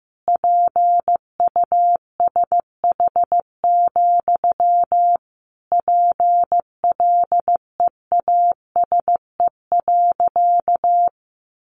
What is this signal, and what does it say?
Morse code, heard as a keyed tone, about 15 wpm, 700 Hz. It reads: PUSH, PLEASE.
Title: Sentences from Top 1500 Words 15wpm